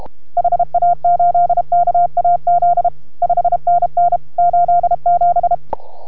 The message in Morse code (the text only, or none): HA8KAZ 5NN 87